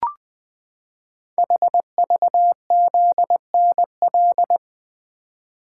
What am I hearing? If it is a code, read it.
Morse code, heard as a keyed tone, about 20 wpm, 700 Hz. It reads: HVZNL
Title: Code Groups - 5 Letters - 20wpm